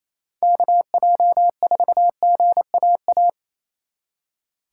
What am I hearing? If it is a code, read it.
Morse code, heard as a keyed tone, about 33 wpm, 700 Hz. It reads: KJ4GAA
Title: Speed Racing - USA 2x3 Call Signs 28wpm